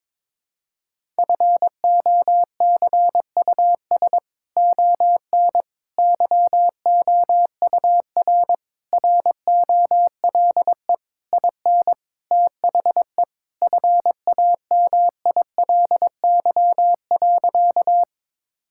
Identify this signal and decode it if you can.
Morse code, heard as a keyed tone, about 22 wpm, 700 Hz. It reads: FOCUS ON YOUR ROLE IN THE FAMILY.
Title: Sentences from Top 1000 Words - Review Part 6 - 22wpm